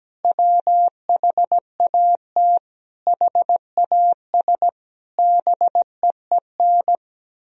Morse code, heard as a keyed tone, about 17 wpm, 700 Hz. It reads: WHAT HAS BEEN
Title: Sentences from Top 1000 Words - Review Part 1 - 17wpm